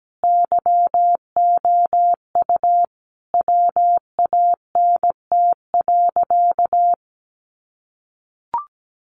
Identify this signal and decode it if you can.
Morse code, heard as a keyed tone, about 17 wpm, 700 Hz. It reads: YOU WANT.
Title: Sentences from Top 200 Words 17wpm